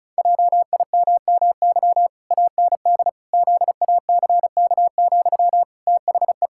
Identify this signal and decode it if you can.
Morse code, heard as a keyed tone, about 36 wpm, 700 Hz. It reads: JIMMY AND ZACK, THE
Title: Pangram Challenge v2 - 35wpm